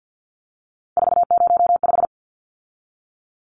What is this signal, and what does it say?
Morse code, heard as a keyed tone, about 50 wpm, 700 Hz. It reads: 405
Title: Numbers - 3-Digit 50wpm